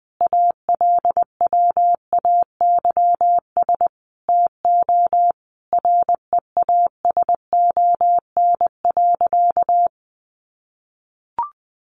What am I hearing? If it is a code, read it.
Morse code, heard as a keyed tone, about 20 wpm, 700 Hz. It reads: ALWAYS TO REASON.
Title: Sentences from Top 1300 Words 20wpm